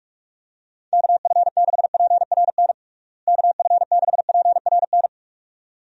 Morse code, heard as a keyed tone, about 46 wpm, 700 Hz. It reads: KF6PRN KF6PRN
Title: USA Call Signs - 6 Characters - Repeated 2x - 45wpm